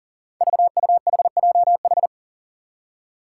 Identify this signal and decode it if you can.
Morse code, heard as a keyed tone, about 40 wpm, 700 Hz. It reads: VUHJH
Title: Five Challenging Letter-Number Combinations 40wpm